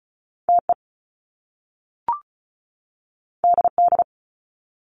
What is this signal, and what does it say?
Morse code, heard as a keyed tone, about 36 wpm, 700 Hz. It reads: TE DD
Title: Top Two Letter Combinations in Words 35wpm